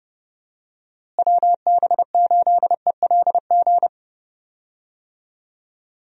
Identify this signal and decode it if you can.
Morse code, heard as a keyed tone, about 30 wpm, 700 Hz. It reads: WB8ELG
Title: USA Call Signs - 6 Characters - Repeated 2x - 30wpm